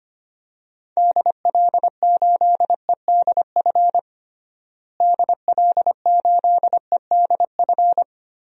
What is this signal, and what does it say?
Morse code, heard as a keyed tone, about 25 wpm, 700 Hz. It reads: DL8EDF DL8EDF
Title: DXCC-100 Call Signs - 6 Characters - Repeated 3x - 25wpm